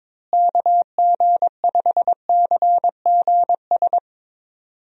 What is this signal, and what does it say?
Morse code, heard as a keyed tone, about 22 wpm, 700 Hz. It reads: KG5CGS
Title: USA Call Signs - 6 Characters - Repeated 3x - 22wpm